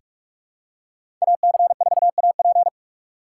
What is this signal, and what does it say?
Morse code, heard as a keyed tone, about 44 wpm, 700 Hz. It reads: AC4AP
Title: USA Call Signs - 5 Characters - 6x Character Spacing - 45wpm